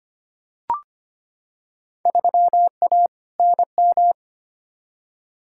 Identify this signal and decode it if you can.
Morse code, heard as a keyed tone, about 25 wpm, 700 Hz. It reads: 3A NM
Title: Field Day Exchanges Encore - 25wpm